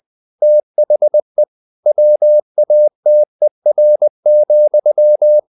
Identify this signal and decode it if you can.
Morse code, heard as a keyed tone, about 20 wpm, 600 Hz. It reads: THE WATER,
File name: the old man and the sea part 2 20wpm